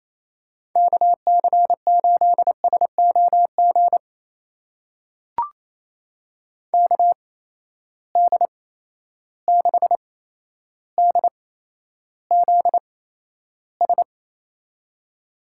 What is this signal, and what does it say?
Morse code, heard as a keyed tone, about 28 wpm, spaced as for 12 wpm, 700 Hz. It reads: KC8SOG KD6DZS
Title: USA Call Signs - 6 Characters - 8x Character Spacing - 28wpm